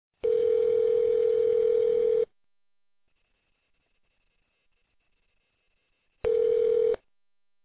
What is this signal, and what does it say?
Ringback tone (North America), 440 + 480 Hz.